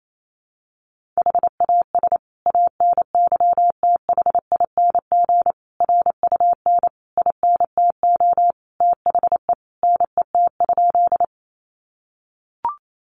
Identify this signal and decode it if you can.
Morse code, heard as a keyed tone, about 28 wpm, 700 Hz. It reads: HAS ANYTHING RUN INTO THE NET?
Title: Sentences from Top 1400 Words 28wpm